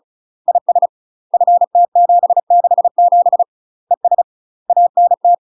Morse code, heard as a keyed tone, about 34 wpm, 700 Hz. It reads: IS FT767 ES ANT
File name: UK MASTER QSOs TRAINING / 35WPM / UK MASTER QSOs 0040